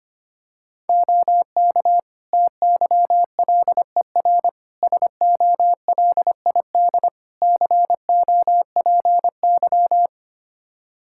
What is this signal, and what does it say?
Morse code, heard as a keyed tone, about 25 wpm, 700 Hz. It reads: OK TYLER SOLID COPY
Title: Practice QSO - Exchange 3 - More Info - 25wpm